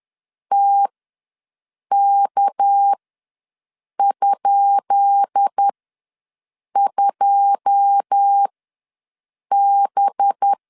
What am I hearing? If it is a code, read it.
Morse code, heard as a keyed tone, about 10 wpm, spaced as for 6 wpm, 800 Hz. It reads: TK?2B
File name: LEZIONI / 10 SIGNS / 10B INT ws 10wpm 4134